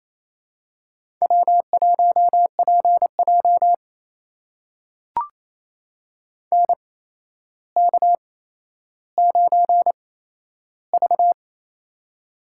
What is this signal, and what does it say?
Morse code, heard as a keyed tone, about 28 wpm, spaced as for 12 wpm, 700 Hz. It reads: W1PJ NK9V
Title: USA Call Signs - 4 Characters - 8x Character Spacing - 28wpm